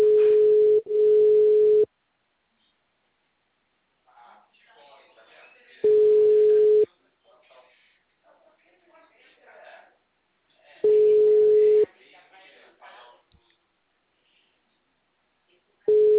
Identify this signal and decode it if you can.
Ringback tone (Europe, 425 Hz), with speech.